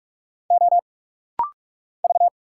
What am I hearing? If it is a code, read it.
Morse code, heard as a keyed tone, about 46 wpm, 700 Hz. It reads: O V
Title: Single Letters - Rapid-Fire 45wpm